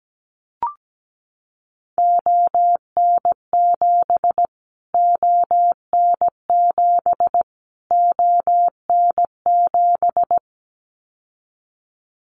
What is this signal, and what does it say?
Morse code, heard as a keyed tone, about 17 wpm, 700 Hz. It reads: ON7 ON7 ON7
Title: Callsign Prefixes for Top 25 DXCC - Repeated 3x - 17wpm